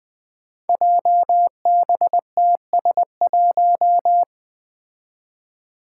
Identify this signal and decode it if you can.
Morse code, heard as a keyed tone, about 20 wpm, 700 Hz. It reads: JBTS1
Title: Five Challenging Letter-Number Combinations 20wpm